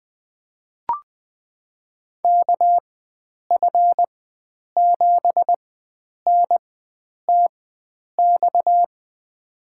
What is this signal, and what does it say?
Morse code, heard as a keyed tone, about 20 wpm, spaced as for 9 wpm, 700 Hz. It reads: KF7NTX
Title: USA Call Signs - 6 Characters - 4x Character Spacing - 20wpm